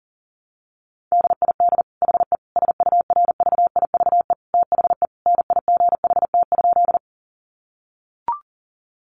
Morse code, heard as a keyed tone, about 40 wpm, 700 Hz. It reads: DID HE SURVIVE THE NIGHT?
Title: Sentences from Top 1300 Words 40wpm